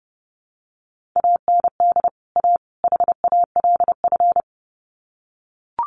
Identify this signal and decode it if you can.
Morse code, heard as a keyed tone, about 30 wpm, 700 Hz. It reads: AND A HALF
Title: Sets of 3 Words 30wpm